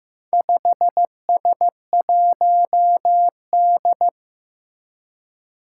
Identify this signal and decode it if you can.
Morse code, heard as a keyed tone, about 15 wpm, 700 Hz. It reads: 5S1D
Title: Four Challenging Letter-Number Combinations 15wpm